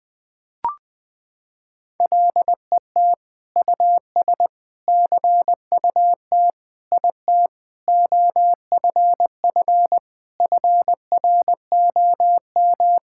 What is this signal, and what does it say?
Morse code, heard as a keyed tone, about 20 wpm, 700 Hz. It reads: LET US CUT IT OFF FROM
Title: Sentences from Top 1000 Words - Review Part 6 - 20wpm